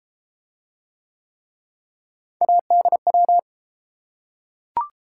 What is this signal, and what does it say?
Morse code, heard as a keyed tone, about 30 wpm, 700 Hz. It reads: ADW